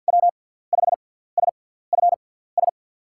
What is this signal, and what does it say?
Morse code, heard as a keyed tone, about 50 wpm, spaced as for 17 wpm, 700 Hz. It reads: W5SFS
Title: USA Call Signs - 6x Character Spacing - 50wpm